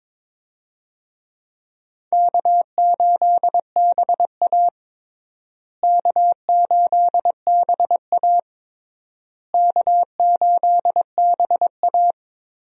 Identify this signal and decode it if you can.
Morse code, heard as a keyed tone, about 22 wpm, 700 Hz. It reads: K8BA K8BA K8BA
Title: USA Call Signs - 4 Characters - Repeated 3x - 22wpm